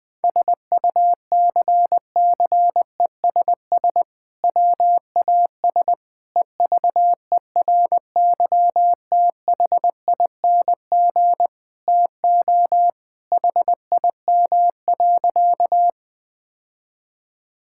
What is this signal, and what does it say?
Morse code, heard as a keyed tone, about 20 wpm, 700 Hz. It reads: SUCCESS WAS EVERYTHING TO HIM.
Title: Sentences from Top 1000 Words - Review Part 6 - 20wpm